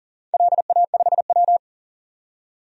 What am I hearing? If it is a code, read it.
Morse code, heard as a keyed tone, about 40 wpm, 700 Hz. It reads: LA5W
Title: DXCC-100 Call Signs - 4 Characters - 40wpm